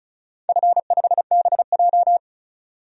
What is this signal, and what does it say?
Morse code, heard as a keyed tone, about 36 wpm, 700 Hz. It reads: F5BJ
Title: DXCC-100 Call Signs - 4 Characters - 35wpm